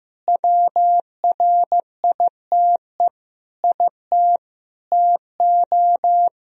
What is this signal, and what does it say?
Morse code, heard as a keyed tone, about 15 wpm, 700 Hz. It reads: WRITE IT TO
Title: Sentences from Top 1000 Words - Review Part 4 - 15wpm